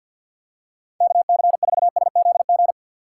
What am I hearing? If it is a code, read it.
Morse code, heard as a keyed tone, about 52 wpm, 690 Hz. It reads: KC4SZG
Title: USA Call Signs - 6 Characters - 3x Character Spacing - 50wpm